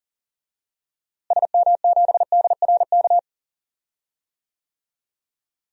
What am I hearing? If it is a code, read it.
Morse code, heard as a keyed tone, about 40 wpm, 700 Hz. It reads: SM7DRK
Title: DXCC-100 Call Signs - 6 Characters - Repeated 2x - 40wpm